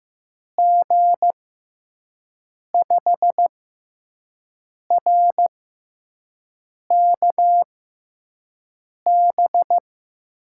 Morse code, heard as a keyed tone, about 15 wpm, spaced as for 5 wpm, 700 Hz. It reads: G5RKB
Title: USA Call Signs - 6 Characters - 6x Character Spacing - 15wpm